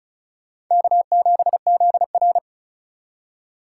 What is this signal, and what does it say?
Morse code, heard as a keyed tone, about 45 wpm, 700 Hz. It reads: K7ZR